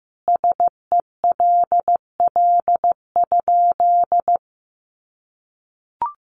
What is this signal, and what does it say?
Morse code, heard as a keyed tone, about 15 wpm, 700 Hz. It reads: SELL?
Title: Sentences from Top 1100 Words 15wpm